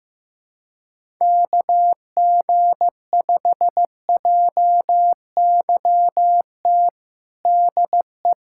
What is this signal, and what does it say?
Morse code, heard as a keyed tone, about 15 wpm, 700 Hz. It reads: KG5JYT DE
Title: USA Callsign DE Callsign 15wpm